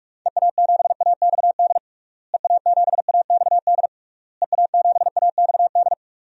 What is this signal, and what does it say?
Morse code, heard as a keyed tone, about 46 wpm, 700 Hz. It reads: EA7AXD EA7AXD EA7AXD
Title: DXCC-100 Call Signs - 6 Characters - Repeated 3x - 45wpm